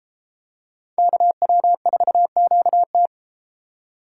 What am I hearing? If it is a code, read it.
Morse code, heard as a keyed tone, about 31 wpm, 700 Hz. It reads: KW4QT